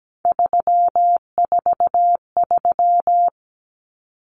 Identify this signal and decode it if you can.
Morse code, heard as a keyed tone, about 17 wpm, 690 Hz. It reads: 343
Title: Common Cut Numbers - 3-Digit 17wpm